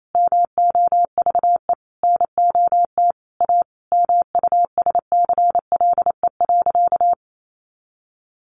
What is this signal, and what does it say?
Morse code, heard as a keyed tone, about 28 wpm, 700 Hz. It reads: MOVE NOT A MUSCLE.
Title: Sentences from Top 1300 Words 28wpm